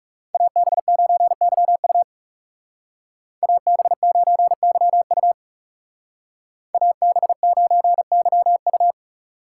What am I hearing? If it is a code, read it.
Morse code, heard as a keyed tone, about 36 wpm, 700 Hz. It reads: AB9YU AB9YU AB9YU
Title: Speed Racing - USA 2x2 Call Signs 30wpm